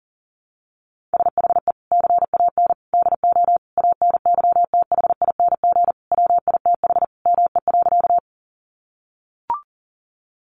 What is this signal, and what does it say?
Morse code, heard as a keyed tone, about 40 wpm, 700 Hz. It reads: SHE CAN DO ANYTHING WITH ME.
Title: Sentences from Top 1000 Words 40wpm